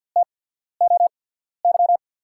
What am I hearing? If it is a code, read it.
Morse code, heard as a keyed tone, about 47 wpm, spaced as for 13 wpm, 700 Hz. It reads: TOY
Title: USA Call Signs - 5 Characters - 8x Character Spacing - 50wpm